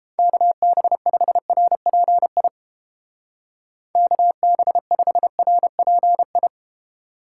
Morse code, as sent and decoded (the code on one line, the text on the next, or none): -.- -... ..... .-. .--. .. / -.- -... ..... .-. .--. ..
KB5RPI KB5RPI